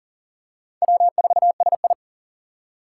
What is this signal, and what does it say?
Morse code, heard as a keyed tone, about 38 wpm, 700 Hz. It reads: W4SI
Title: USA Call Signs - 3x Character Spacing - 40wpm